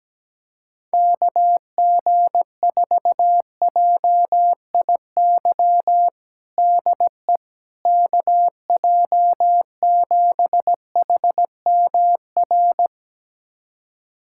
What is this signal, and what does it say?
Morse code, heard as a keyed tone, about 17 wpm, 700 Hz. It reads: KG4JIY DE KJ7HMR
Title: USA Callsign DE Callsign 17wpm